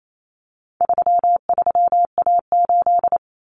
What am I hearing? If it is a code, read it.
Morse code, heard as a keyed tone, about 28 wpm, 700 Hz. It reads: 33A8
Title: Common Cut Numbers - 4-Digit 28wpm